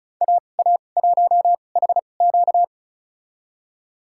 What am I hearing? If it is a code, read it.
Morse code, heard as a keyed tone, about 34 wpm, spaced as for 25 wpm, 700 Hz. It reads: AA1HQ
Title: USA Call Signs - 5 Characters - 2x Character Spacing - 35wpm